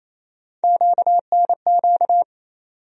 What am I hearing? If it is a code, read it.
Morse code, heard as a keyed tone, about 28 wpm, 700 Hz. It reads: QNQ
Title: USA Call Sign Postfixes - 3 Letters 28wpm